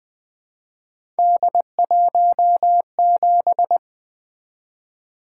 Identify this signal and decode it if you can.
Morse code, heard as a keyed tone, about 20 wpm, 700 Hz. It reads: D17
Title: Three Letter-Number Combinations - 20wpm